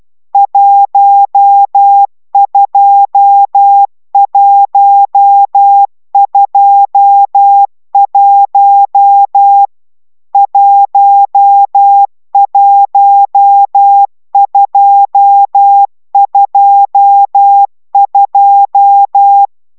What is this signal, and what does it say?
Morse code, heard as a keyed tone, about 12 wpm, 800 Hz. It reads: 12121 11222
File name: taviro tanulas / cw 69 22 10